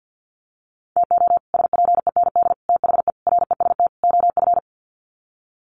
Morse code, heard as a keyed tone, about 50 wpm, 700 Hz. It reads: TO SPEND THE REST OF
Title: Sets of 5 Words Encore2 50wpm